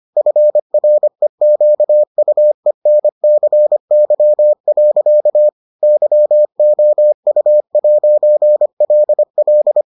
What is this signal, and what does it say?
Morse code, heard as a keyed tone, about 25 wpm, 600 Hz. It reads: FREQUENCY. YOU'LL